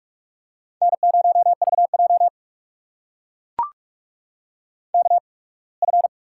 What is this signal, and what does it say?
Morse code, heard as a keyed tone, about 43 wpm, spaced as for 26 wpm, 700 Hz. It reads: N0VJ KF
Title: USA Call Signs - 8x Character Spacing - 45wpm